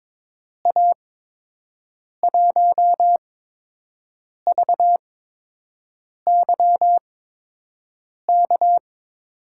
Morse code, heard as a keyed tone, about 22 wpm, spaced as for 6 wpm, 700 Hz. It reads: A1VYK
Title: USA Call Signs - 6 Characters - 8x Character Spacing - 22wpm